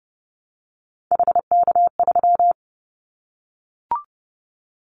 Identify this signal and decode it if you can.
Morse code, heard as a keyed tone, about 28 wpm, 700 Hz. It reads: HK3
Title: Callsign Prefixes for Top 100 DXCC - 30wpm